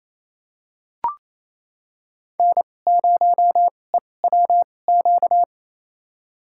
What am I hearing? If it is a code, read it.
Morse code, heard as a keyed tone, about 28 wpm, spaced as for 20 wpm, 700 Hz. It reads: N0EWQ
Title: USA Call Signs - 5 Characters - 2x Character Spacing - 28wpm